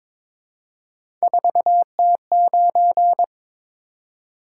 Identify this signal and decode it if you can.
Morse code, heard as a keyed tone, about 23 wpm, 700 Hz. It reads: 4T9
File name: Three Letter-Number Combinations - 22wpm